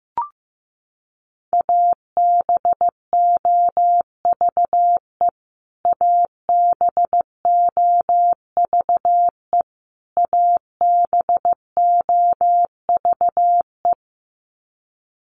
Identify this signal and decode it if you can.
Morse code, heard as a keyed tone, about 15 wpm, 700 Hz. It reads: ABOVE ABOVE ABOVE
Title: Top 500 Words Repeated 15wpm